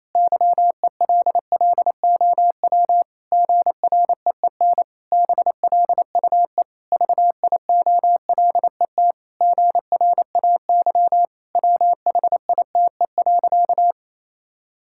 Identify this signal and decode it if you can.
Morse code, heard as a keyed tone, about 28 wpm, 700 Hz. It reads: YELLOW GREEN BLUE VIOLET GRAY WHITE.
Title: Pangram Challenge v2 - 28wpm